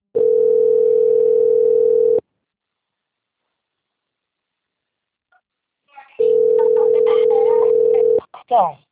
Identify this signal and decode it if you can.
Ringback tone (North America), 440 + 480 Hz, with speech.
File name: out- - - - - .21688